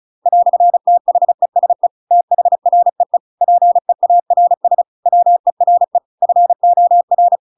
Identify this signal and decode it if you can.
Morse code, heard as a keyed tone, about 34 wpm, 700 Hz. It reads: "THESE THREE PEARS WERE FOR
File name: Pinocchio / Pinocchio ch7 35 35 ik1hge